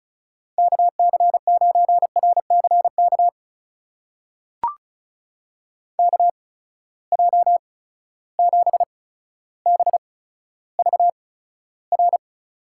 Morse code, heard as a keyed tone, about 36 wpm, spaced as for 15 wpm, 700 Hz. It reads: KC9RCK KJ7BVR
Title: USA Call Signs - 6 Characters - 8x Character Spacing - 35wpm